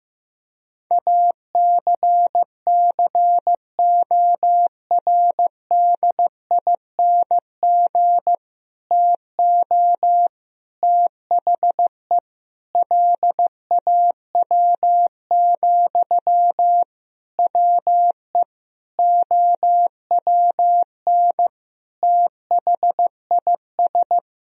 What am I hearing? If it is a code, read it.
Morse code, heard as a keyed tone, about 15 wpm, 700 Hz. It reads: ACCORDING TO THE LAW, WE OWN THIS
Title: Sentences from Top 1000 Words - Review Part 4 - 15wpm